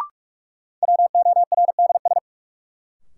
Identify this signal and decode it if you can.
Morse code, heard as a keyed tone, about 46 wpm, 700 Hz. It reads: WORDS
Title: Sentences from Top 1000 Words - Review Part 6 - New Word Drill - 45wpm